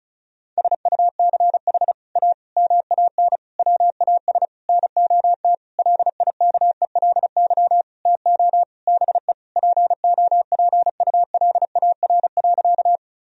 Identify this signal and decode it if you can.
Morse code, heard as a keyed tone, about 36 wpm, 700 Hz. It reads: SUCH A MAN WAS NOT LIKELY TO BE POPULAR.
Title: Sentences from Top 1000 Words - Review Part 4 - 35wpm